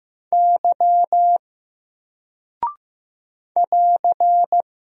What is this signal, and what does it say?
Morse code, heard as a keyed tone, about 15 wpm, 700 Hz. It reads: Y <AR>
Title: Single Character (Letter:Number:Punctuation:Prosign) Alt 15wpm